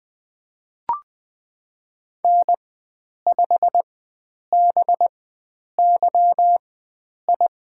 Morse code, heard as a keyed tone, about 20 wpm, spaced as for 9 wpm, 700 Hz. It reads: N5BYI
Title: USA Call Signs - 5 Characters - 4x Character Spacing - 20wpm